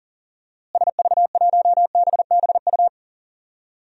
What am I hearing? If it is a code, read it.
Morse code, heard as a keyed tone, about 40 wpm, 700 Hz. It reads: SV1BBU
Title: DXCC-100 Call Signs - 6 Characters - Repeated 2x - 40wpm